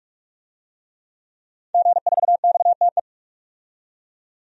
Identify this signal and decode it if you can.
Morse code, heard as a keyed tone, about 46 wpm, 700 Hz. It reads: G4XTE
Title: DXCC-100 Call Signs - 5 Characters - 45wpm